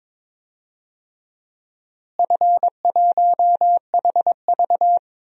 Morse code, heard as a keyed tone, about 22 wpm, 700 Hz. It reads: F1HV
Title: DXCC-100 Call Signs - 4 Characters - Repeated 3x - 22wpm